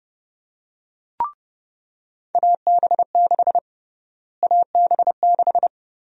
Morse code, heard as a keyed tone, about 32 wpm, 700 Hz. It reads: AB6 AB6
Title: USA Call Sign Prefixes - Repeated 2x - 30wpm